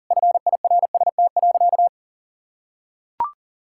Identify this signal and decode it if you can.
Morse code, heard as a keyed tone, about 37 wpm, 700 Hz. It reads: FIRST.
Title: Sentences from Top 1000 Words - Review Part 6 - 40wpm